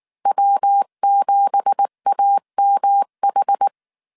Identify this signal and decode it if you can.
Morse code, heard as a keyed tone, about 19 wpm, 800 Hz. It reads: W7AMH
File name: CW / DET / 1 9 1500C S C 18wpm 2539